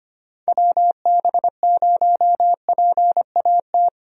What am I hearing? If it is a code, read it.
Morse code, heard as a keyed tone, about 25 wpm, 700 Hz. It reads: WB0PAT
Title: USA Call Signs - 6 Characters - 25wpm